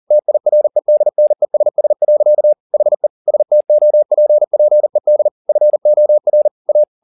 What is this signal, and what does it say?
Morse code, heard as a keyed tone, about 40 wpm, 600 Hz. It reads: TIREDNESS. HE STOPPED FOR A
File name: the old man and the sea 2 40wpm